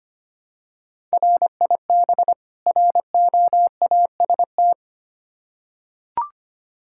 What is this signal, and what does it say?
Morse code, heard as a keyed tone, about 25 wpm, 700 Hz. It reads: RIB ROAST
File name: practice / 25wpm / Difficult Words with Hints - 25wpm